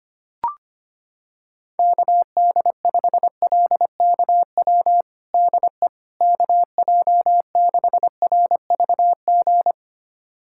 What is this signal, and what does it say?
Morse code, heard as a keyed tone, about 25 wpm, 700 Hz. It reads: KD5LKW DE KJ6RVG
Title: USA Callsign DE Callsign - 6 Characters - 25wpm